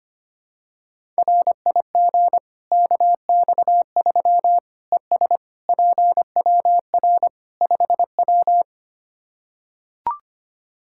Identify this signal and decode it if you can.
Morse code, heard as a keyed tone, about 25 wpm, 700 Hz. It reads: RIG KX3 ES PWR 5W
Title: Practice QSO - Exchange 2 - Info - 25wpm